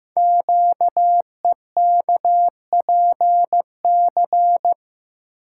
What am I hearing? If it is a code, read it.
Morse code, heard as a keyed tone, about 15 wpm, 700 Hz. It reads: QEKPC
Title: Code Groups - 5 Letters - 15wpm